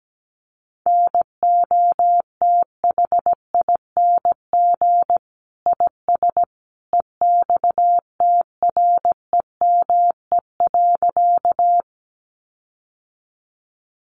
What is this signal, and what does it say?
Morse code, heard as a keyed tone, about 17 wpm, 700 Hz. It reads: NOTHING IS EXTREME.
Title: Sentences from Top 1300 Words 17wpm